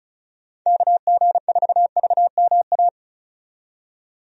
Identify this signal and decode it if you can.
Morse code, heard as a keyed tone, about 36 wpm, 700 Hz. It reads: KG4VMA
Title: USA Call Signs - 6 Characters - 35wpm